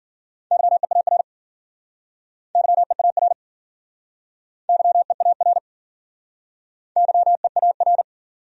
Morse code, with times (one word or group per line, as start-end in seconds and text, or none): 0.5-1.2 YEAR
2.5-3.3 YEAR
4.7-5.6 YEAR
7.0-8.0 YEAR